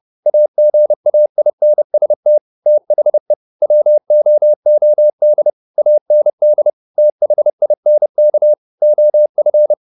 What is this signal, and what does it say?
Morse code, heard as a keyed tone, about 30 wpm, 600 Hz. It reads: AGAINST THE WOOD AND THINK OF
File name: the old man and the sea / the old man and the sea 1 30wpm